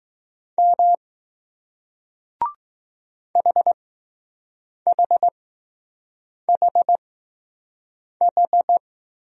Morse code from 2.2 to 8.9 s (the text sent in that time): HHHH